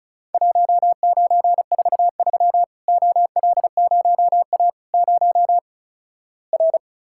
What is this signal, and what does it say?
Morse code, heard as a keyed tone, about 36 wpm, 700 Hz. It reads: 1943 OL0A 0 R
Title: Worked All Europe DX Contest - QTC Exchange - 35wpm